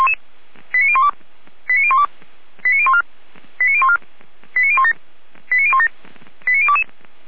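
Five-tone selective call (ZVEI), heard as quick runs of five tones, tones of about 70 ms. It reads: 89012 89012 89014 89014 89017 89017 89022
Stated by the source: Five-tone ZVEI selective call with code 89020, received on an SDR.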